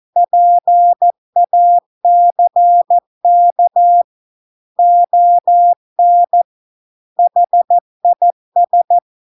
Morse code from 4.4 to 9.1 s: ON HIS